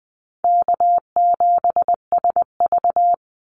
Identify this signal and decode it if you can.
Morse code, heard as a keyed tone, about 19 wpm, 700 Hz. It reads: K7SV
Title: Call Signs - 20wpm